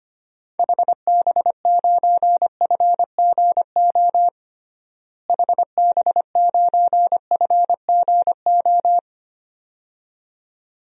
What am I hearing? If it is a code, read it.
Morse code, heard as a keyed tone, about 25 wpm, 700 Hz. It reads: HB9FGO HB9FGO
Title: DXCC-100 Call Signs - 6 Characters - Repeated 2x - 25wpm